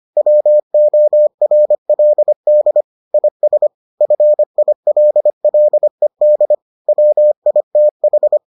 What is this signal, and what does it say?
Morse code, heard as a keyed tone, about 25 wpm, 600 Hz. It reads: WORLD IS FILLED WITH